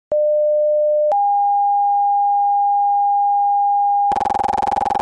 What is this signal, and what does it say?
Two-tone page, heard as a one-second tone, then a three-second tone, about 600 then 810 Hz. It is heard from 0.1 to 4.2 s.